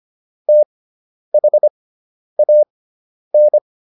Morse code, heard as a keyed tone, about 25 wpm, spaced as for 10 wpm, 600 Hz. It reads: THAN